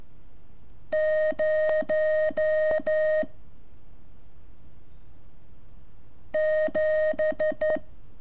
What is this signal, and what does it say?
Morse code, heard as a keyed tone, about 14 wpm, 620 Hz. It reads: 0 7
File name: curso / ejercicio 28